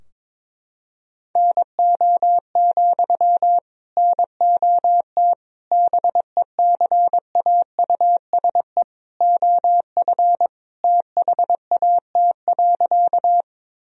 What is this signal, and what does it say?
Morse code, heard as a keyed tone, about 22 wpm, 700 Hz. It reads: NO, NOT BECAUSE OF THAT.
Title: Sentences from Top 1000 Words - Review Part 3 - 22wpm